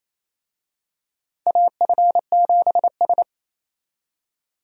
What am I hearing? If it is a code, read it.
Morse code, heard as a keyed tone, about 27 wpm, 700 Hz. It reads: AF7S
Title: USA Call Signs - 4 Characters - 28wpm